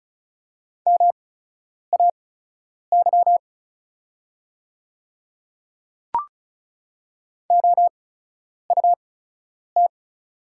Morse code, heard as a keyed tone, about 36 wpm, spaced as for 10 wpm, 700 Hz. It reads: MAY OUT
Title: 8x Character Spacing - Top 100 Words - 35wpm